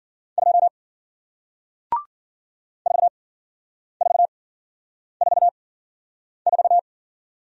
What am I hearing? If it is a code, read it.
Morse code, heard as a keyed tone, about 53 wpm, spaced as for 9 wpm, 700 Hz. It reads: 2 4444